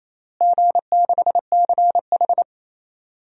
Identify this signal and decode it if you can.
Morse code, heard as a keyed tone, about 28 wpm, 700 Hz. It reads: G6CH
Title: DXCC-100 Call Signs - 4 Characters - Repeated 2x - 28wpm